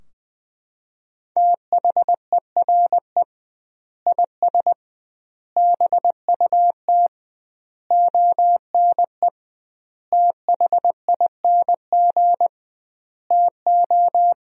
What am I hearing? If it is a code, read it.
Morse code, heard as a keyed tone, about 21 wpm, 700 Hz. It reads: THERE IS BUT ONE THING TO
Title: 2x Word-Spacing - Sentences from Top 100 Words - 20wpm